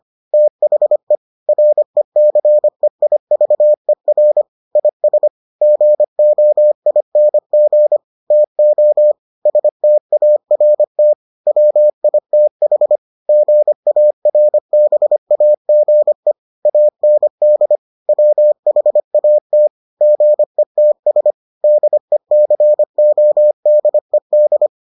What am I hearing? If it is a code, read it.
Morse code, heard as a keyed tone, about 25 wpm, 600 Hz. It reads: THE RECEIVER IS GOING TO START WITH GARBAGE AND WHAT GETS DECODED